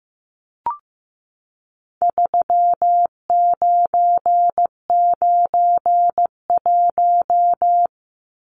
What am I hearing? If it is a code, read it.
Morse code, heard as a keyed tone, about 15 wpm, 700 Hz. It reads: 3991
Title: Numbers - 4-Digit 15wpm